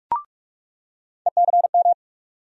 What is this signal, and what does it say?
Morse code, heard as a keyed tone, about 46 wpm, 700 Hz. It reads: ECM
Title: USA Call Sign Postfixes - 3 Letters 45wpm